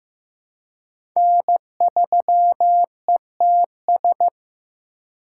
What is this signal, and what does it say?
Morse code, heard as a keyed tone, about 15 wpm, 700 Hz. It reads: N3ETS
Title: USA Call Signs - 5 Characters - 3x Character Spacing - 15wpm